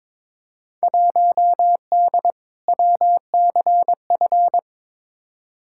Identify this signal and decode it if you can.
Morse code, heard as a keyed tone, about 22 wpm, 700 Hz. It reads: 1D WCF
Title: Field Day Exchanges - 22wpm